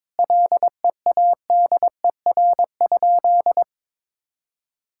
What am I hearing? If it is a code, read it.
Morse code, heard as a keyed tone, about 22 wpm, 700 Hz. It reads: LEADER?
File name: Sentences with 7-Word Phrases - 22wpm